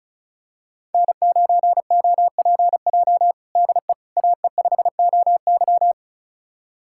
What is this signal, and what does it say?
Morse code, heard as a keyed tone, about 36 wpm, 700 Hz. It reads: N9OPJ DE AE5OY
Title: USA Callsign DE Callsign - 5 Characters - 35wpm